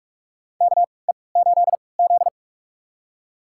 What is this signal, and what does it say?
Morse code, heard as a keyed tone, about 46 wpm, spaced as for 25 wpm, 700 Hz. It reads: KE8Z
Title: USA Call Signs - 4 Characters - 3x Character Spacing - 45wpm